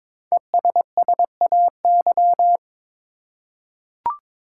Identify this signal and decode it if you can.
Morse code, heard as a keyed tone, about 22 wpm, 700 Hz. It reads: ESSAY
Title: Five Letter Words - 22wpm